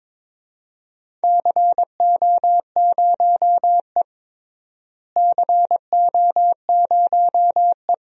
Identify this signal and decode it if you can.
Morse code, heard as a keyed tone, about 22 wpm, 700 Hz. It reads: CO0E CO0E
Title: DXCC-100 Call Signs - 4 Characters - Repeated 3x - 22wpm